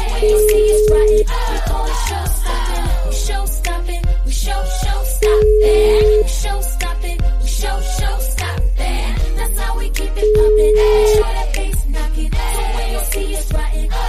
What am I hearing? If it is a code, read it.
Ringback tone (Europe, 425 Hz).